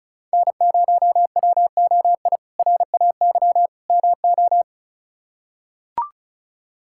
Morse code, heard as a keyed tone, about 34 wpm, 700 Hz. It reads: N0WOI RAY MO
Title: K1USN SST QSO Practice - 35wpm